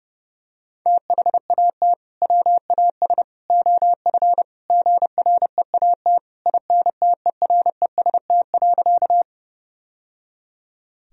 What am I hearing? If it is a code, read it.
Morse code, heard as a keyed tone, about 30 wpm, 700 Hz. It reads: THAT WAS OF GREAT INTEREST.
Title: Sentences from Top 1000 Words - Review Part 2 - 30wpm